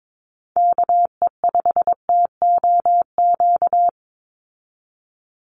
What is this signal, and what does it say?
Morse code, heard as a keyed tone, about 22 wpm, 700 Hz. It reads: KE5TOQ
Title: USA Call Signs 22wpm